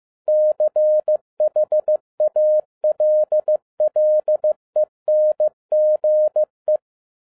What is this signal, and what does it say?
Morse code, heard as a keyed tone, about 15 wpm, 610 Hz. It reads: CHALLENGE
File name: 2011-07-mp3 / n07 A LOT HAS CHANGED 15wpm